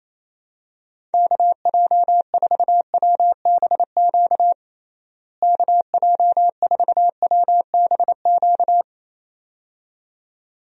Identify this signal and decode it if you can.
Morse code, heard as a keyed tone, about 28 wpm, 700 Hz. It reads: KJ4WBQ KJ4WBQ
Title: USA Call Signs - 6 Characters - Repeated 2x - 28wpm